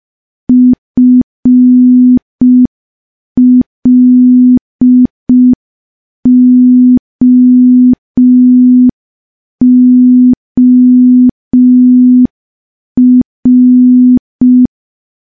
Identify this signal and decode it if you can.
Morse code, heard as a keyed tone, about 5 wpm, 250 Hz. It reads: FLOOR
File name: morse / morse-training-05WPM